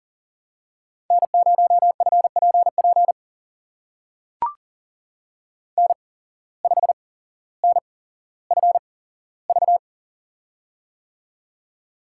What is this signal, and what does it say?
Morse code, heard as a keyed tone, about 39 wpm, spaced as for 17 wpm, 700 Hz. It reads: N0FPP N5NFV
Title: USA Call Signs - 5 Characters - 8x Character Spacing - 40wpm